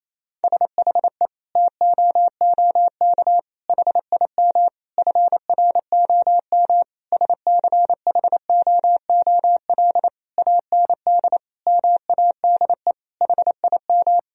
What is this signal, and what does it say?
Morse code, heard as a keyed tone, about 28 wpm, 700 Hz. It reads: SHE TOOK HIM FROM SCHOOL AND MADE HIM
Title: Sentences from Top 1000 Words - Review Part 4 - 28wpm